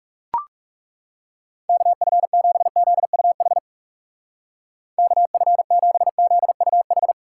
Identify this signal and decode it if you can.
Morse code, heard as a keyed tone, about 41 wpm, 700 Hz. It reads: KF7ZUH KF7ZUH